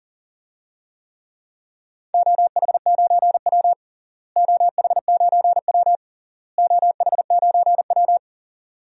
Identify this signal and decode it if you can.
Morse code, heard as a keyed tone, about 40 wpm, 700 Hz. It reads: OH9W OH9W OH9W
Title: DXCC-100 Call Signs - 4 Characters - Repeated 3x - 40wpm